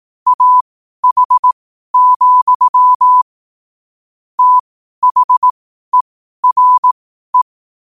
Morse code, heard as a keyed tone, about 18 wpm, spaced as for 12 wpm, 1000 Hz. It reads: AH, THERE
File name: 12wpm